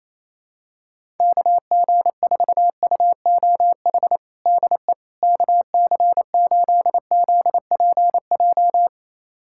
Morse code, heard as a keyed tone, about 28 wpm, 700 Hz. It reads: KG4UOH DE KC8ZPJ
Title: USA Callsign DE Callsign - 6 Characters - 28wpm